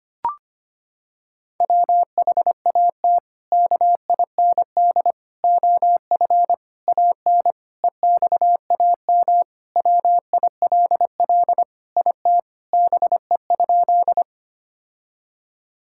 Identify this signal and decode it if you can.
Morse code, heard as a keyed tone, about 25 wpm, 700 Hz. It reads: WHAT KIND OF AN EXAM WILL IT BE?
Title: Sentences from Top 1000 Words - Review Part 5 - 25wpm